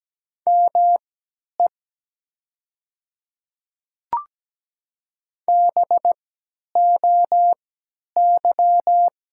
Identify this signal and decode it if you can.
Morse code, heard as a keyed tone, about 17 wpm, spaced as for 10 wpm, 700 Hz. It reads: ME BOY
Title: 3x Character Spacing - Top 200 Words - 17wpm